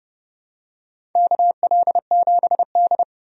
Morse code, heard as a keyed tone, about 30 wpm, 700 Hz. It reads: KL7D
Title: DXCC-100 Call Signs - 4 Characters - 30wpm